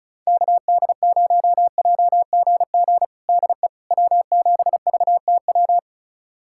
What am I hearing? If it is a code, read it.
Morse code, heard as a keyed tone, about 36 wpm, 700 Hz. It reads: KD0JGG DE W7VTW
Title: USA Callsign DE Callsign 35wpm